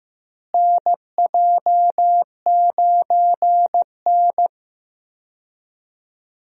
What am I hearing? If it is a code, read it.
Morse code, heard as a keyed tone, about 16 wpm, 700 Hz. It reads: NJ9N
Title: USA Call Signs - 4 Characters - Repeated 2x - 15wpm